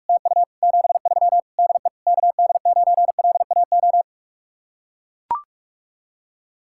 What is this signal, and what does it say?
Morse code, heard as a keyed tone, about 44 wpm, 700 Hz. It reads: TU 73 DE KD9LAO